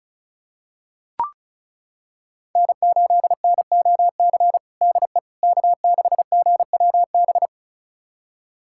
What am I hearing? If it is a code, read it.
Morse code, heard as a keyed tone, about 36 wpm, 700 Hz. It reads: N8NOC DE K6GWB
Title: USA Callsign DE Callsign - 5 Characters - 35wpm